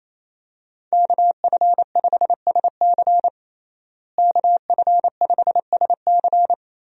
Morse code, heard as a keyed tone, about 28 wpm, 700 Hz. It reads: KF5SC KF5SC
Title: USA Call Signs - 5 Characters - Repeated 2x - 28wpm